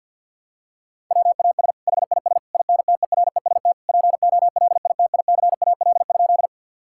Morse code, heard as a keyed tone, about 50 wpm, 700 Hz. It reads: WAS HIS INTEREST POLITICAL?
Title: Sentences from Top 1000 Words - Review Part 5 - 50wpm